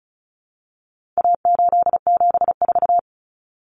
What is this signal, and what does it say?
Morse code, heard as a keyed tone, about 34 wpm, 700 Hz. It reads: A874
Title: Common Cut Numbers - 4-Digit 35wpm